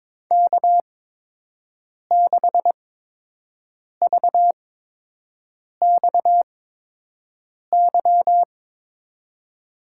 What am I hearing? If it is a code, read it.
Morse code, heard as a keyed tone, about 22 wpm, spaced as for 6 wpm, 700 Hz. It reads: K6VXY
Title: USA Call Signs - 8x Character Spacing - 22wpm